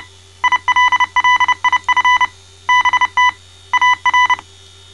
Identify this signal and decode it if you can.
Morse code, heard as a keyed tone, about 29 wpm, 1000 Hz. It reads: ILLIF BT AR
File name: mrx / malta